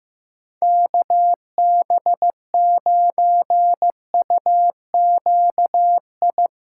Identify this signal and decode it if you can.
Morse code, heard as a keyed tone, about 15 wpm, 700 Hz. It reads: KB9UQI
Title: USA Callsign DE Callsign - 6 Characters - 15wpm